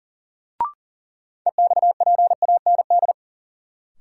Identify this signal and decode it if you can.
Morse code, heard as a keyed tone, about 40 wpm, 700 Hz. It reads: EXPAND
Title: Sentences from Top 1400 Words - New Word Drill - 40wpm